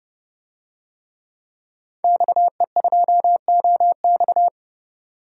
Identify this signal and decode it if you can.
Morse code, heard as a keyed tone, about 30 wpm, 700 Hz. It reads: XE2OX
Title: DXCC-100 Call Signs - 5 Characters - 30wpm